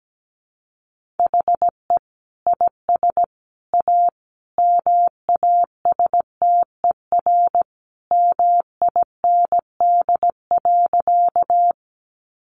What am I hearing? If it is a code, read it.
Morse code, heard as a keyed tone, about 17 wpm, 700 Hz. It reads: HE IS A MASTER MIND.
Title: Sentences from Top 1300 Words 17wpm